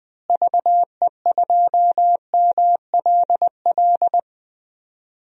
Morse code, heard as a keyed tone, about 20 wpm, 700 Hz. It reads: VE2MLL
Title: DXCC-100 Call Signs - 6 Characters - Repeated 2x - 20wpm